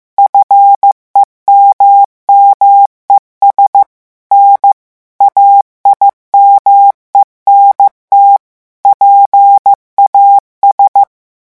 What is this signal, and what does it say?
Morse code, heard as a keyed tone, about 15 wpm, 800 Hz. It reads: FEMMES N AIMENT PAS
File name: morse mp3 / revision9 citations humoristiques vitesse 14 mots minute